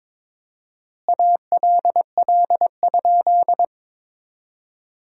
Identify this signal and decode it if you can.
Morse code, heard as a keyed tone, about 22 wpm, 700 Hz. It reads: ALL?
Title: 4x Word-Spacing - Sentences from Top 100 Words - 22wpm